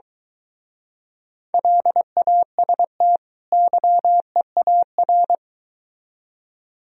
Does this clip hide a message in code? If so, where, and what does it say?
Yes: Morse code from 1.5 to 5.4 s, heard as a keyed tone, about 23 wpm, 700 Hz. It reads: LAST YEAR